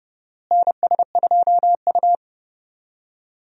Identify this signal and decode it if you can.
Morse code, heard as a keyed tone, about 30 wpm, 700 Hz. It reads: NS2U
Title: USA Call Signs - 4 Characters - Repeated 2x - 30wpm